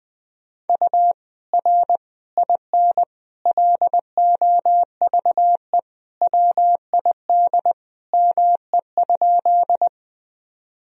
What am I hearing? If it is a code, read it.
Morse code, heard as a keyed tone, about 20 wpm, 700 Hz. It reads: U R IN LOVE WID ME?
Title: Sentences from Top 400 Words and Abbreviations 20wpm